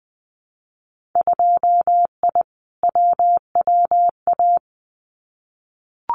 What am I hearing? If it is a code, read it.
Morse code, heard as a keyed tone, about 20 wpm, 700 Hz. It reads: 2I WWA
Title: Winter Field Day Exchanges Encore 20wpm